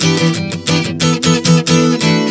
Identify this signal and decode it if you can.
There is no code.